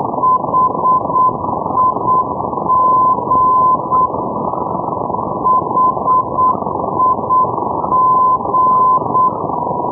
Morse code, heard as a keyed tone, about 8 wpm, 1020 Hz, with noise. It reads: HIG HIG
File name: europe / d-3500 ndb bremen hig-318-hh-2012-03-rb